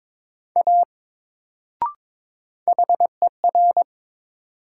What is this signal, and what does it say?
Morse code, heard as a keyed tone, about 22 wpm, 700 Hz. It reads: A HER
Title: Sentences from Top 100 Words - New Word Drill - 22wpm